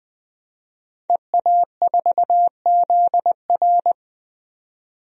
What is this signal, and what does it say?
Morse code, heard as a keyed tone, about 20 wpm, 700 Hz. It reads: EA4ZR
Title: DXCC-100 Call Signs - 5 Characters - 20wpm